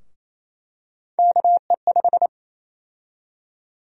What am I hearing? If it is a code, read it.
Morse code, heard as a keyed tone, about 28 wpm, 700 Hz. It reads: KE5
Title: USA Call Sign Prefixes - 28wpm